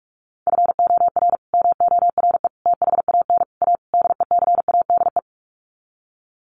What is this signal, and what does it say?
Morse code, heard as a keyed tone, about 44 wpm, 700 Hz. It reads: FOR MORE THAN A DECADE
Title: Sets of 5 Words 45wpm